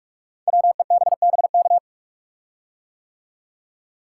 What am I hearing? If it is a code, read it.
Morse code, heard as a keyed tone, about 44 wpm, 700 Hz. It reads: WEBBK
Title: Code Groups - 5 Letter:Numbers - 45wpm